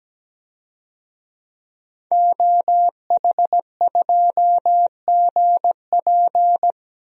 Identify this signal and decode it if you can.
Morse code, heard as a keyed tone, about 17 wpm, 700 Hz. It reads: OH2GP
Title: DXCC-100 Call Signs - 5 Characters - Repeated 2x - 17wpm